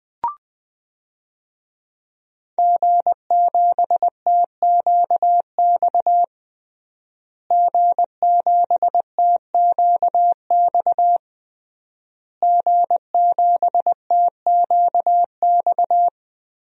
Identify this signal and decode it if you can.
Morse code, heard as a keyed tone, about 20 wpm, 700 Hz. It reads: G7TQX G7TQX G7TQX
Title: DXCC-100 Call Signs - 5 Characters - Repeated 3x - 20wpm